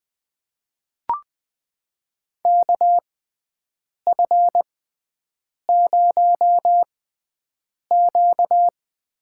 Morse code, heard as a keyed tone, about 21 wpm, spaced as for 7 wpm, 700 Hz. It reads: KF0Q
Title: USA Call Signs - 6x Character Spacing - 20wpm